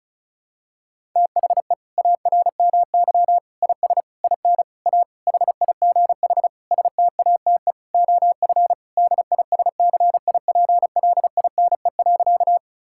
Morse code, heard as a keyed tone, about 36 wpm, 700 Hz. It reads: THE ARMY IS IN A HIGH STATE OF DISCIPLINE.
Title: Sentences from Top 1000 Words - Review Part 6 - 35wpm